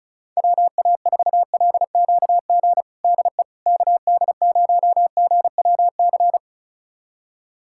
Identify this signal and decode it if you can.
Morse code, heard as a keyed tone, about 36 wpm, 700 Hz. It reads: WA4LQG DE KD0GWC
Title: USA Callsign DE Callsign - 6 Characters - 35wpm